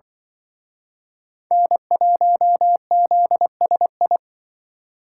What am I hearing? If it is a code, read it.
Morse code, heard as a keyed tone, about 25 wpm, 700 Hz. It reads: N1ZSI